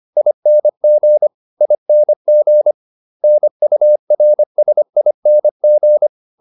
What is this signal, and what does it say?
Morse code, heard as a keyed tone, about 25 wpm, 600 Hz. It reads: ING ING NURSING